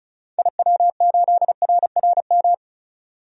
Morse code, heard as a keyed tone, about 36 wpm, 700 Hz. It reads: IW8RRM
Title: DXCC-100 Call Signs - 6 Characters - 35wpm